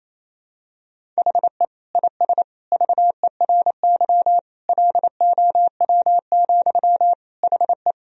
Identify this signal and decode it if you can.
Morse code, heard as a keyed tone, about 28 wpm, 700 Hz. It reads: HE IS VERY LOW, HE
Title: Sentences from Top 1000 Words - Review Part 1 - 28wpm